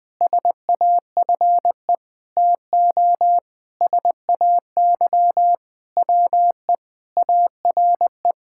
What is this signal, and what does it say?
Morse code, heard as a keyed tone, about 20 wpm, 700 Hz. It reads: SAFE TO SAY WE ARE
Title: Sentences with 6-Word Phrases - 20wpm